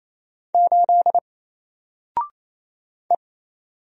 Morse code, heard as a keyed tone, about 27 wpm, 700 Hz. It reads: 8 E
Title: Single Letter-Number 28wpm